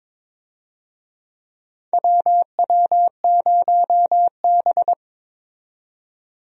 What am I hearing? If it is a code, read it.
Morse code, heard as a keyed tone, about 22 wpm, 700 Hz. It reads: WW0B
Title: USA Call Signs - 4 Characters - 22wpm